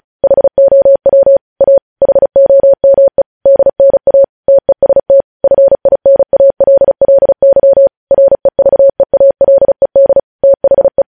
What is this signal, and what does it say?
Morse code, heard as a keyed tone, about 36 wpm, 550 Hz. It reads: HOW A HOME DNA TEST FINALLY REVEALED THE
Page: news headlines in morse code at 35 wpm